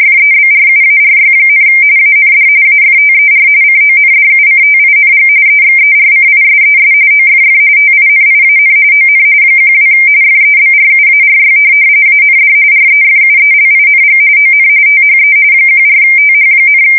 RTTY, about 45.45 baud. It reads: FEBRUARY, AND AFTER CONDUCTING EXERCISES OFF TINIAN ON 12 FEBRUARY AND REFUELING ON THE 14TH,